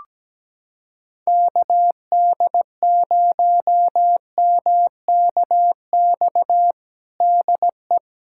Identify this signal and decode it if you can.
Morse code, heard as a keyed tone, about 17 wpm, 700 Hz. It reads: KD0MKX DE
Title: USA Callsign DE Callsign - 6 Characters - 17wpm